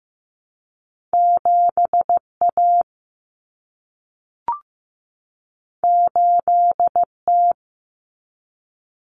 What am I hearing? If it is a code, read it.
Morse code, heard as a keyed tone, about 15 wpm, 700 Hz. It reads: 7A 8T
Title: Common Cut Numbers - 2-Digit 15wpm